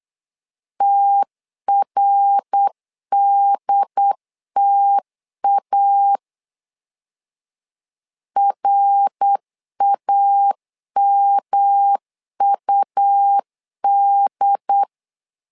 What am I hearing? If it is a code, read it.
Morse code, heard as a keyed tone, about 8 wpm, 800 Hz. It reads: TRDTA RAMUD
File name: LEZIONI / 3 MWRO / 3 MWR 8wpm 2444